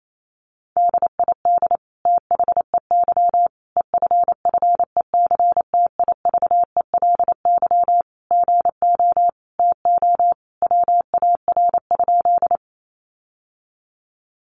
Morse code, heard as a keyed tone, about 28 wpm, 700 Hz. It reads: DID THEY EFFECTIVELY GO TO WAR?
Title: Sentences from Top 1400 Words 28wpm